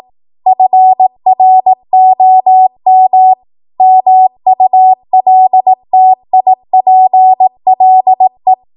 Morse code, heard as a keyed tone, about 18 wpm, 750 Hz. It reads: FROM MULTIPLE
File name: Morse / 18WPM